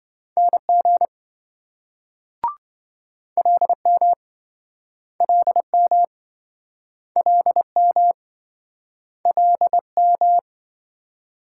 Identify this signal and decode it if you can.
Morse code, heard as a keyed tone, about 24 wpm, 700 Hz. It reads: NG LM LM LM LM
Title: Speed-Racing Multi-letter Phonemes - 20wpm